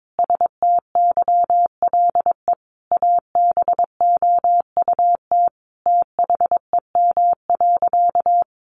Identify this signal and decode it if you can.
Morse code, heard as a keyed tone, about 22 wpm, 700 Hz. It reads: STYLE ABOUT THEM.
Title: Sentences from Top 800 Words 22wpm